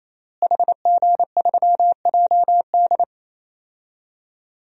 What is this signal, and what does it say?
Morse code, heard as a keyed tone, about 28 wpm, 700 Hz. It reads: HG3JD
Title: DXCC-100 Call Signs - 5 Characters - Repeated 3x - 28wpm